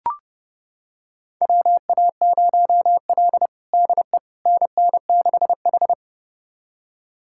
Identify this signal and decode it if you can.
Morse code, heard as a keyed tone, about 30 wpm, 700 Hz. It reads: WA0L DE NN6H
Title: USA Callsign DE Callsign - 4 Characters - 30wpm